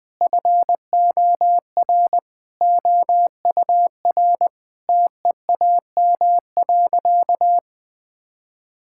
Morse code, heard as a keyed tone, about 20 wpm, 700 Hz. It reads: FOR OUR TEAM.
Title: Sentences with 8-Word Phrases - Spoken Beginning - 20wpm